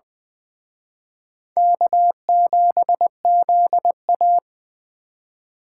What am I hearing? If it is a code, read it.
Morse code, heard as a keyed tone, about 24 wpm, 700 Hz. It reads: K7ZA
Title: Speed Racing - USA 1x2 Call Signs 20wpm